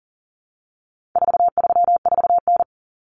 Morse code, heard as a keyed tone, about 40 wpm, 700 Hz. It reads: 434N
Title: Common Cut Numbers - 4-Digit 40wpm